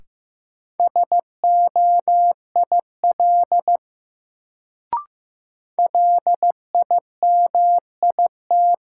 Morse code, heard as a keyed tone, about 15 wpm, 700 Hz. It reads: SOIL LIMIT
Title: Sentences from Top 1500 Words - New Word Drill Alt - 15wpm